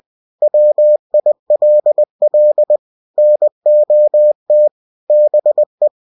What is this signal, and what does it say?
Morse code, heard as a keyed tone, about 20 wpm, 600 Hz. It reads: WILL NOT BE